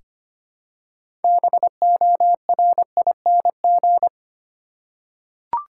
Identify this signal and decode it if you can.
Morse code, heard as a keyed tone, about 25 wpm, 700 Hz. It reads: BORING
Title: Six Letter Words - 25wpm